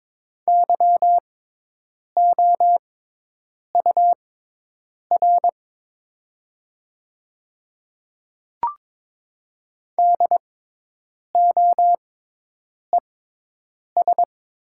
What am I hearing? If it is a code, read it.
Morse code, heard as a keyed tone, about 22 wpm, spaced as for 8 wpm, 700 Hz. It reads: YOUR DOES
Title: 6x Character Spacing - Top 200 Words - 22wpm